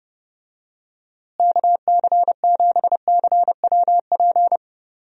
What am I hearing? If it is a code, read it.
Morse code, heard as a keyed tone, about 32 wpm, 700 Hz. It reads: KC7CWP
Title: USA Call Signs - 6 Characters - Repeated 3x - 30wpm